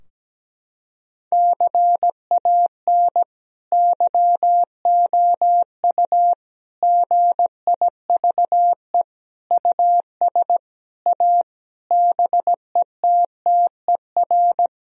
Morse code, heard as a keyed tone, about 17 wpm, 700 Hz. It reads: CAN YOU GIVE US A BETTER
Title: Sentences from Top 1000 Words - Review Part 4 - 17wpm